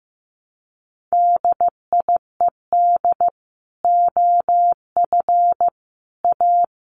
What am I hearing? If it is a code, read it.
Morse code, heard as a keyed tone, about 15 wpm, 700 Hz. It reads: DIED OF A
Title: Sets of 5 Words Encore2 15wpm